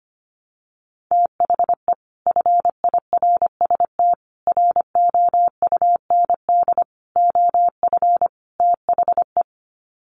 Morse code, heard as a keyed tone, about 25 wpm, 700 Hz. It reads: THE FIRST ROUND OF THE
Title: Sets of 5 Words Encore 25wpm